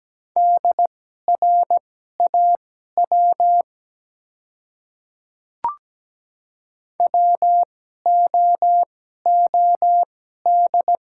Morse code, heard as a keyed tone, about 17 wpm, spaced as for 12 wpm, 700 Hz. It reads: DRAW WOOD
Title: 2x Character Spacing - Top 300 Words - 17wpm